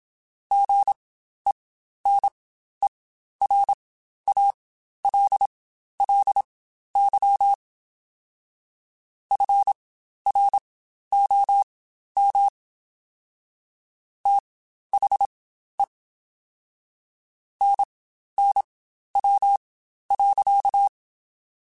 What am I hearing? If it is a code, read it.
Morse code, heard as a keyed tone, about 27 wpm, spaced as for 12 wpm, 800 Hz. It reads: GENERALLY FROM THE NNW.
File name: Spaced Characters / Spaced Characters-mp3s / Weather 5 12 sp @25